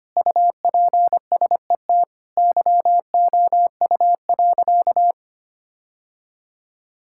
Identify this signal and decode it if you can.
Morse code, heard as a keyed tone, about 25 wpm, 700 Hz. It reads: UPSET YOU.
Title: Sentences from Top 1000 Words - Review Part 5 - 25wpm